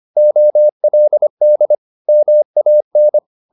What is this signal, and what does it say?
Morse code, heard as a keyed tone, about 25 wpm, 600 Hz. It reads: OLD MAN
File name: the old man and the sea part 1 25wpm